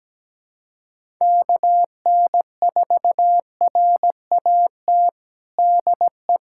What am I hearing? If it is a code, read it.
Morse code, heard as a keyed tone, about 17 wpm, 700 Hz. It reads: KN4RAT DE
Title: USA Callsign DE Callsign - 6 Characters - 17wpm